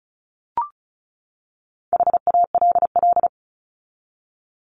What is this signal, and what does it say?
Morse code, heard as a keyed tone, about 34 wpm, 700 Hz. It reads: HALL
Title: Top 1751-2000 Words 35wpm